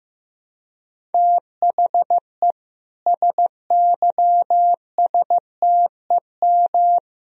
Morse code, heard as a keyed tone, about 15 wpm, 700 Hz. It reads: THE SYSTEM
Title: Sentences from Top 1000 Words - Review Part 3 - 15wpm